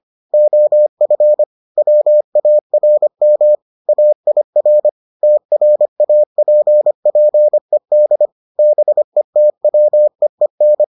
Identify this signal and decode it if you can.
Morse code, heard as a keyed tone, about 25 wpm, 600 Hz. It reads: OF WARM AIR TRAPPED BETWEEN